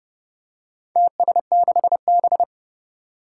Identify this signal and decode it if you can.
Morse code, heard as a keyed tone, about 30 wpm, 700 Hz. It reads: TS6B
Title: Four Challenging Letter-Number Combinations 30wpm